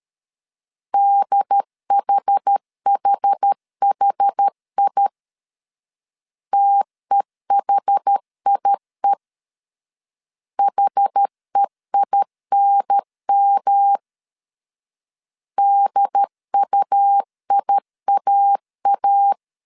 Morse code, heard as a keyed tone, about 13 wpm, 800 Hz. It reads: DHHHI TEHIE HEINM DUIAA
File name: CW / DET / 1 3 1500C S B 12wpm 2828